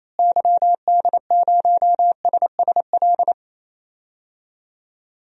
Morse code, heard as a keyed tone, about 27 wpm, 700 Hz. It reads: YD0SSL
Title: DXCC-100 Call Signs - 6 Characters - Repeated 2x - 28wpm